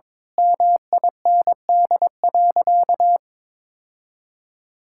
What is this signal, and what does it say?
Morse code, heard as a keyed tone, about 23 wpm, 700 Hz. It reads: MIND.
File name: Difficult Sentences with Hints - 22wpm